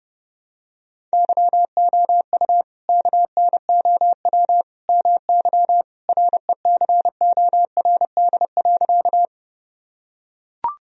Morse code, heard as a keyed tone, about 30 wpm, 700 Hz. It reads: YOU KNOW MY RECORD.
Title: Sentences from Top 1000 Words - Review Part 3 - 30wpm